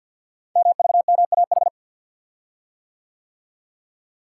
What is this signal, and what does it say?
Morse code, heard as a keyed tone, about 52 wpm, 700 Hz. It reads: MVMAH
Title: Code Groups - 5 Letters - 50wpm